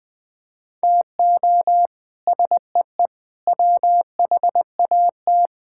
Morse code, heard as a keyed tone, about 21 wpm, 700 Hz. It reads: TO SEE WHAT
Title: Sets of 8 Words - 20wpm